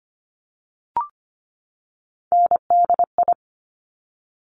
Morse code, heard as a keyed tone, about 25 wpm, 700 Hz. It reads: NDI
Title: Top Three Letter Combinations in Words 25wpm